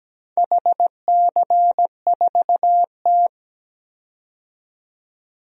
Morse code, heard as a keyed tone, about 17 wpm, 700 Hz. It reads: HC4T
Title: DXCC-100 Call Signs - 4 Characters - 17wpm